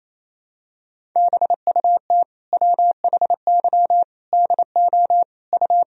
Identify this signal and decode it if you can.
Morse code, heard as a keyed tone, about 28 wpm, 700 Hz. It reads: BUT WHY DO U
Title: Sentences from Top 200 Words and Abbreviations 28wpm